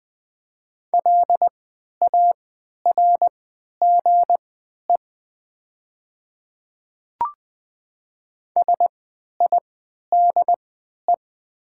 Morse code, heard as a keyed tone, about 20 wpm, spaced as for 11 wpm, 700 Hz. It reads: LARGE SIDE
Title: 3x Character Spacing - Top 200 Words - 20wpm